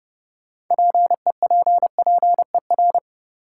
Morse code, heard as a keyed tone, about 32 wpm, 700 Hz. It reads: PEPPER
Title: Sets of 5 Words - New Word Drill - 30wpm